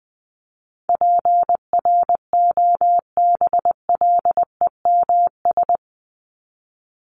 Morse code, heard as a keyed tone, about 20 wpm, 700 Hz. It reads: PROBLEMS
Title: ES:S Suffix - Top 100 Words with Base Words 20wpm